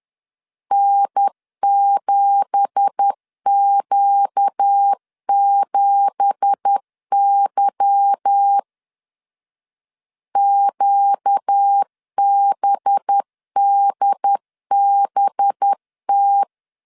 Morse code, heard as a keyed tone, about 10 wpm, 800 Hz. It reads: N7Q7Y QBDBT
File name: CW / DET / 1 9 1500C S A 10wpm 4511